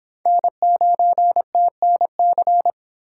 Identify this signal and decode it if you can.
Morse code, heard as a keyed tone, about 22 wpm, 700 Hz. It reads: N9TNC